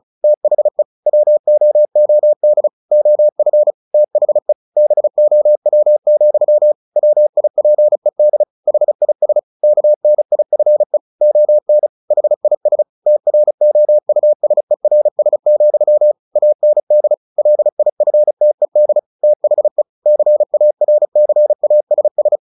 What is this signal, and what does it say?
Morse code, heard as a keyed tone, about 34 wpm, 600 Hz. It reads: THE WOOD OF THE BOW, WIPED HIS KNIFE ON HIS TROUSERS, AND LIFTED THE CARCASS
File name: the old man and the sea / the old man and the sea 1 35wpm